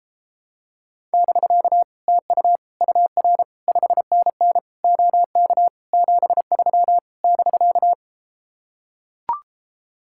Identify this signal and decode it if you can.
Morse code, heard as a keyed tone, about 33 wpm, 700 Hz. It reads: <BK> TU UR 5NN OK 73 <BK>